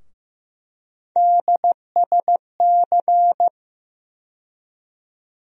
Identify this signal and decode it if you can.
Morse code, heard as a keyed tone, about 16 wpm, 700 Hz. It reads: DSC
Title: USA Call Sign Postfixes - 3 Letters 15wpm